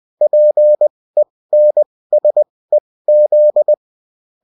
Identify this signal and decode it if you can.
Morse code, heard as a keyed tone, about 20 wpm, spaced as for 16 wpm, 600 Hz. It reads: PENSEZ
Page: bulletin-uft-sous-forme-texte-et-en-cw